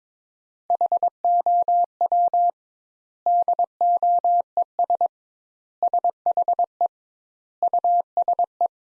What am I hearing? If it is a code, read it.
Morse code, heard as a keyed tone, about 22 wpm, 700 Hz. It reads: HOW DOES SHE USE
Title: Sentences Top 200 Words - 22wpm